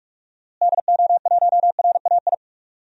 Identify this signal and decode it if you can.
Morse code, heard as a keyed tone, about 43 wpm, 700 Hz. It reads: DO1RAI
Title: DXCC-100 Call Signs - 6 Characters - Repeated 2x - 45wpm